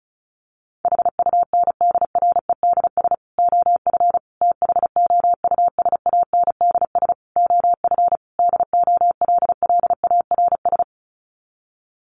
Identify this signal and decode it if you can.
Morse code, heard as a keyed tone, about 34 wpm, 700 Hz. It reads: HUNDREDS OF THOUSANDS OF DOLLARS
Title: Sets of 5 Words Encore 35wpm